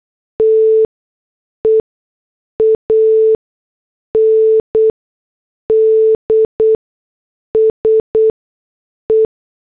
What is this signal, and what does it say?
Morse code, heard as a keyed tone, about 8 wpm, spaced as for 6 wpm, 440 Hz. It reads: TE AND SE
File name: morse / headlines